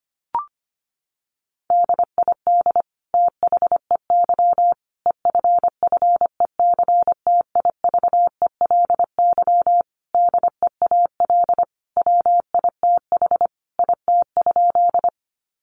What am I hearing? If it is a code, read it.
Morse code, heard as a keyed tone, about 25 wpm, 700 Hz. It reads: DID THEY EFFECTIVELY DEAL WITH IT?
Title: Sentences from Top 1400 Words 25wpm